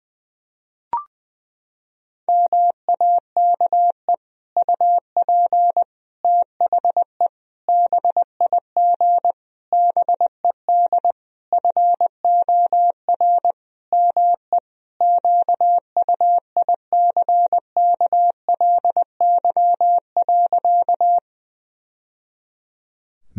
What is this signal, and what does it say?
Morse code, heard as a keyed tone, about 20 wpm, 700 Hz. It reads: MAKE UP THE BIG BED FOR ME QUICKLY.
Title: Sentences from Top 1000 Words - Review Part 1 - 20wpm